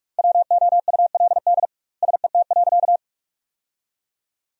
Morse code, heard as a keyed tone, about 46 wpm, 700 Hz. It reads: WOULD SET.
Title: 2x Word-Spacing - Sentences from Top 400 Words - 45wpm